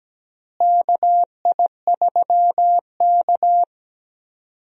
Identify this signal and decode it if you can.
Morse code, heard as a keyed tone, about 17 wpm, 700 Hz. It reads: KI3K
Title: USA Call Signs - 4 Characters - Repeated 3x - 17wpm